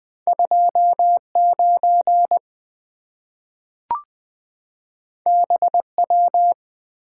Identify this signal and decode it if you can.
Morse code, heard as a keyed tone, about 20 wpm, 700 Hz. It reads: 29 BW
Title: Two Letter-Number Combinations - 20wpm